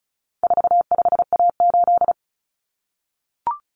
Morse code, heard as a keyed tone, about 34 wpm, 700 Hz. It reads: 45A8
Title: Common Cut Numbers - 4-Digit 35wpm